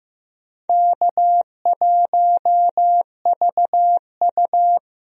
Morse code, heard as a keyed tone, about 15 wpm, 700 Hz. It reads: K1VU
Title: USA Call Signs - 4 Characters - Repeated 3x - 15wpm